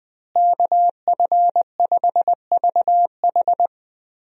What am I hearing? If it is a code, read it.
Morse code, heard as a keyed tone, about 20 wpm, 700 Hz. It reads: KF5VH
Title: USA Call Signs - 5 Characters - Repeated 3x - 20wpm